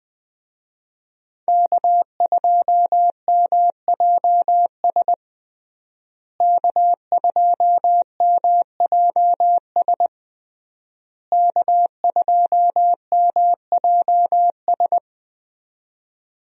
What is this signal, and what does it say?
Morse code, heard as a keyed tone, about 21 wpm, 700 Hz. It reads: K2MJS K2MJS K2MJS
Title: USA Call Signs - 5 Characters - Repeated 3x - 20wpm